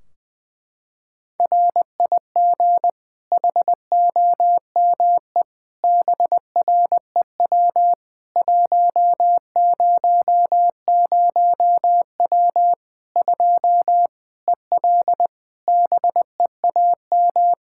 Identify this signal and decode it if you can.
Morse code, heard as a keyed tone, about 20 wpm, 700 Hz. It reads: RIG HOME BREW 100W 2 EL BEAM
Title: Practice QSO - Exchange 2 - Info - 20wpm